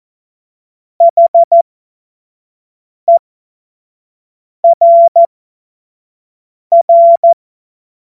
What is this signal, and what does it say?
Morse code, heard as a keyed tone, about 14 wpm, spaced as for 5 wpm, 680 Hz. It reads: HERR
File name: Morsekurs / Witze / Witz 23 April 2024 25 F70